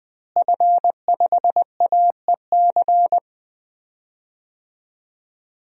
Morse code, heard as a keyed tone, about 21 wpm, 700 Hz. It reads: F5AEC
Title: DXCC-100 Call Signs - 5 Characters - Repeated 3x - 20wpm